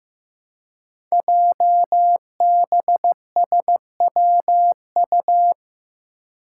Morse code, heard as a keyed tone, about 15 wpm, 700 Hz. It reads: JBSWU
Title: Code Groups - 5 Letter:Numbers - 15wpm